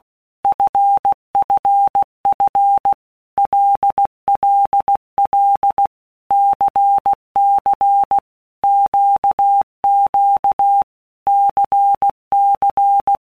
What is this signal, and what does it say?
Morse code, heard as a keyed tone, about 16 wpm, 800 Hz. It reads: FFF LLL CC QQ CC
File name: Basic Letter Groups By Blocks / Groups@15 / LFCYQ@15